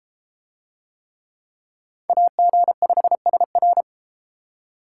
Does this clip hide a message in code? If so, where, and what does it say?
Yes: Morse code from 2.1 to 3.8 s, heard as a keyed tone, about 32 wpm, 700 Hz. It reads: AG5SR